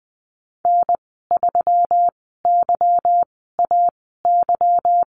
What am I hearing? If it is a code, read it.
Morse code, heard as a keyed tone, about 20 wpm, spaced as for 15 wpm, 700 Hz. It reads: N3YAY
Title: 147 Lesson 24g - Letter y - Call Signs 20wpm